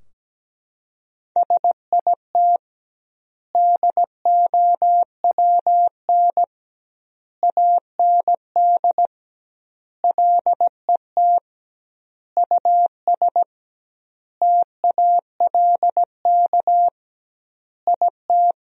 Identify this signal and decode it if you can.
Morse code, heard as a keyed tone, about 17 wpm, 700 Hz. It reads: SIT DOWN AND LET US TALK IT
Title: 2x Word-Spacing - Sentences from Top 500 Words - 17wpm